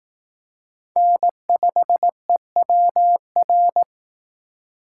Morse code, heard as a keyed tone, about 18 wpm, 700 Hz. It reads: N5EWR
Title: Speed Racing - USA 1x3 Call Signs 15wpm